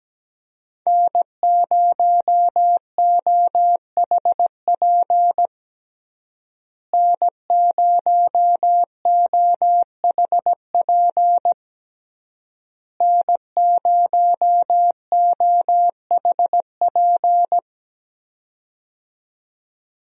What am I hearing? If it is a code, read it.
Morse code, heard as a keyed tone, about 17 wpm, 700 Hz. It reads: N0OHP N0OHP N0OHP
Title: USA Call Signs - 5 Characters - Repeated 3x - 17wpm